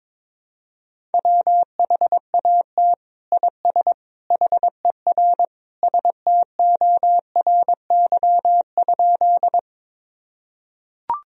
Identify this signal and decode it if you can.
Morse code, heard as a keyed tone, about 21 wpm, 700 Hz. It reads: WHAT IS HER STORY?
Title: Sentences from Top 1000 Words - Review Part 1 - 22wpm